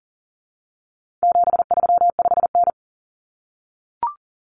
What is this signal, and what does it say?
Morse code, heard as a keyed tone, about 40 wpm, 700 Hz. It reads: 735N
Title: Common Cut Numbers - 4-Digit 40wpm